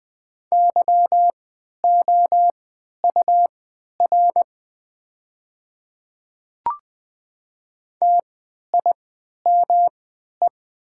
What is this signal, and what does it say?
Morse code, heard as a keyed tone, about 20 wpm, spaced as for 11 wpm, 700 Hz. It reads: YOUR TIME
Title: 3x Character Spacing - Top 300 Words - 20wpm